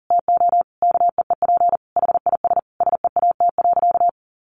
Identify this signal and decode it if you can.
Morse code, heard as a keyed tone, about 40 wpm, 700 Hz. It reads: TO KEEP HIS SEAT.
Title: Sentences from Top 1500 Words 40wpm